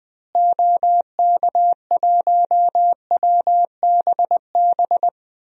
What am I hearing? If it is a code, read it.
Morse code, heard as a keyed tone, about 20 wpm, 700 Hz. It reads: OK1WBB
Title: DXCC-100 Call Signs - 6 Characters - Repeated 2x - 20wpm